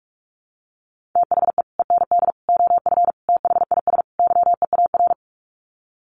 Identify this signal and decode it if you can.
Morse code, heard as a keyed tone, about 44 wpm, 700 Hz. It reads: THE END OF THIS YEAR
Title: Sets of 5 Words Encore 45wpm